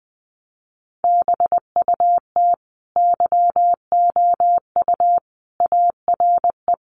Morse code, heard as a keyed tone, about 20 wpm, 700 Hz. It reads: BUT YOU ARE
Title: Sentences from Top 1200 Words 20wpm